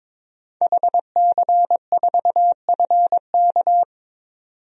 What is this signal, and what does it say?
Morse code, heard as a keyed tone, about 21 wpm, 700 Hz. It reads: HC4FK
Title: DXCC-100 Call Signs - 5 Characters - 22wpm